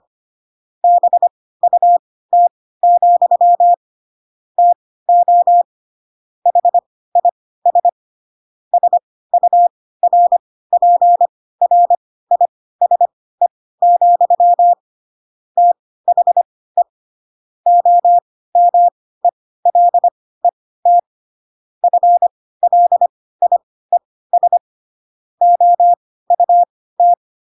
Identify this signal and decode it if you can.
Morse code, heard as a keyed tone, about 25 wpm, spaced as for 16 wpm, 700 Hz. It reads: BUT, TO HIS SURPRISE, THE OMELET FLIES OUT